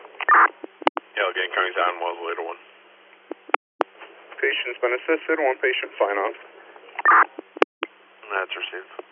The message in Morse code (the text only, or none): none